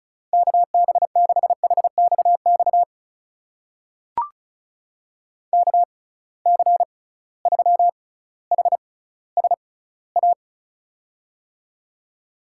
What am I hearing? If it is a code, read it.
Morse code, heard as a keyed tone, about 36 wpm, spaced as for 18 wpm, 700 Hz. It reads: KB6HXX KC3HSA
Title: USA Call Signs - 6 Characters - 6x Character Spacing - 35wpm